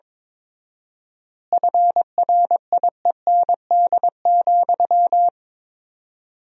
Morse code, heard as a keyed tone, about 22 wpm, 700 Hz. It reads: FRIEND,
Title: 4x Word-Spacing - Sentences from Top 500 Words - 22wpm